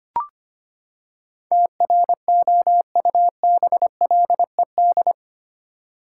Morse code, heard as a keyed tone, about 25 wpm, 700 Hz. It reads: TROUBLED
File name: Eight Letter Words - 25wpm